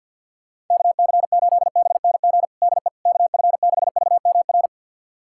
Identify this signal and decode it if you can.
Morse code, heard as a keyed tone, about 51 wpm, 700 Hz. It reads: KC8BNG DE KF6VMR
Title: USA Callsign DE Callsign - 6 Characters - 50wpm